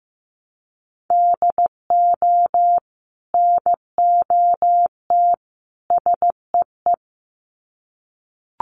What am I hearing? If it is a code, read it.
Morse code, heard as a keyed tone, about 15 wpm, 700 Hz. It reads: DO NOT SEE
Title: Sets of 3 Words 15wpm